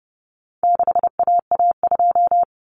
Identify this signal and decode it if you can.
Morse code, heard as a keyed tone, about 30 wpm, 700 Hz. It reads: 6AA2
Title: Common Cut Numbers - 4-Digit 30wpm